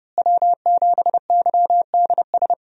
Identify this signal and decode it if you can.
Morse code, heard as a keyed tone, about 30 wpm, 700 Hz. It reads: W7YDS
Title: USA Call Signs - 5 Characters - Repeated 3x - 30wpm